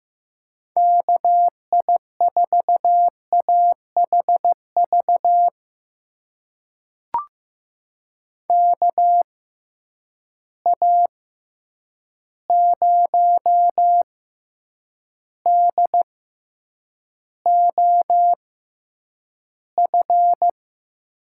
Morse code, heard as a keyed tone, about 15 wpm, spaced as for 8 wpm, 700 Hz. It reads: KI4AHV KA0DOF
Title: USA Call Signs - 6 Characters - 6x Character Spacing - 15wpm